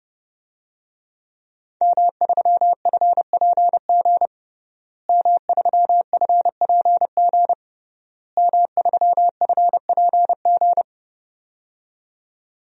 Morse code, heard as a keyed tone, about 30 wpm, 700 Hz. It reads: M3FPG M3FPG M3FPG
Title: DXCC-100 Call Signs - 5 Characters - Repeated 3x - 30wpm